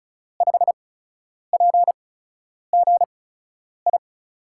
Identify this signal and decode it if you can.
Morse code, heard as a keyed tone, about 36 wpm, spaced as for 10 wpm, 700 Hz. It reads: 5PGI
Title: USA Call Signs - 6 Characters - 8x Character Spacing - 35wpm